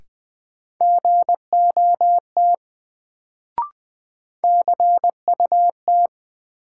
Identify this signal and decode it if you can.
Morse code, heard as a keyed tone, about 20 wpm, 700 Hz. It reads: GOT CUT
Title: Sentences from Top 300 Words - New Word Drill - 20wpm